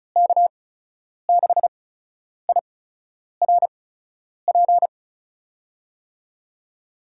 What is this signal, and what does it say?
Morse code, heard as a keyed tone, about 36 wpm, 700 Hz. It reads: K 6 I R P
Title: USA Call Signs - 8x Character Spacing - 35wpm